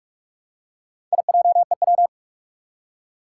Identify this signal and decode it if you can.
Morse code, heard as a keyed tone, about 44 wpm, 700 Hz. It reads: IJEW
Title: Four Challenging Letter-Number Combinations 45wpm